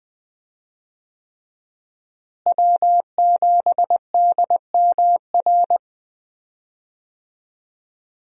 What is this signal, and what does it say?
Morse code, heard as a keyed tone, about 20 wpm, 700 Hz. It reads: W7DMR
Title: USA Call Signs - 5 Characters - Repeated 3x - 20wpm